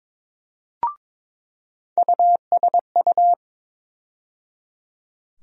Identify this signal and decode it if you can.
Morse code, heard as a keyed tone, about 22 wpm, 700 Hz. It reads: USU
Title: USA Call Sign Postfixes - 3 Letters 22wpm